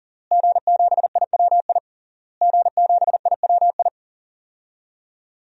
Morse code, heard as a keyed tone, about 40 wpm, 700 Hz. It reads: G7IWI G7IWI
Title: DXCC-100 Call Signs - 5 Characters - Repeated 3x - 40wpm